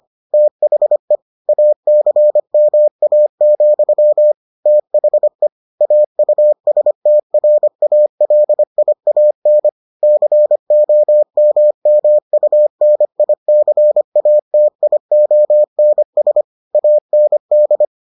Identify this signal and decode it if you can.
Morse code, heard as a keyed tone, about 25 wpm, 600 Hz. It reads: THE ACMA, THE AUSTRALIAN COMMUNICATIONS AND